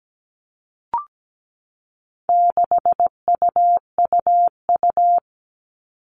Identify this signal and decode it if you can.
Morse code, heard as a keyed tone, about 17 wpm, 700 Hz. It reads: 6UUU
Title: All Cut Numbers - 4-Digit 17wpm